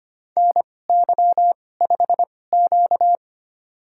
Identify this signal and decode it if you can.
Morse code, heard as a keyed tone, about 25 wpm, spaced as for 18 wpm, 700 Hz. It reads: NY5Q
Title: USA Call Signs - 4 Characters - 2x Character Spacing - 25wpm